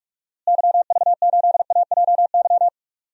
Morse code, heard as a keyed tone, about 41 wpm, 700 Hz. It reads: YV8AJY
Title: DXCC-100 Call Signs - 6 Characters - Repeated 3x - 45wpm